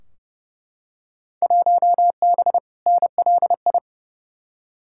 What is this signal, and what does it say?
Morse code, heard as a keyed tone, about 32 wpm, 700 Hz. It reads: 1B NLI
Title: Field Day Exchanges Encore - 30wpm